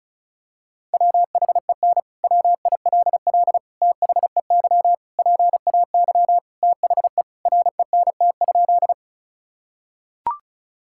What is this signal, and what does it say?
Morse code, heard as a keyed tone, about 34 wpm, 700 Hz. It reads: WHEN WILL THEY PAY THE RENT?
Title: Sentences from Top 1000 Words - Review Part 5 - 35wpm